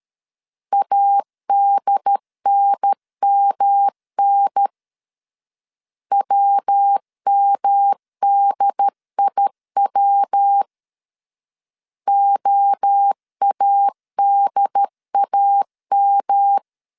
Morse code, heard as a keyed tone, about 13 wpm, 800 Hz. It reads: ADNMN WMDIW OADAM
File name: CW / DET / 1 3 1500C S B 12wpm 2828